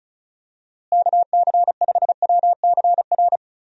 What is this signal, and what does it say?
Morse code, heard as a keyed tone, about 34 wpm, 700 Hz. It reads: KC5WCR
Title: USA Call Signs - 6 Characters - Repeated 2x - 35wpm